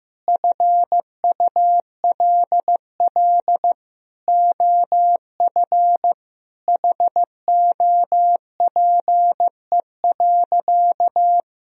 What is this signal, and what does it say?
Morse code, heard as a keyed tone, about 15 wpm, 700 Hz. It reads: FULL OF HOPE.
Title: Sentences from Top 1000 Words - Review Part 3 - 15wpm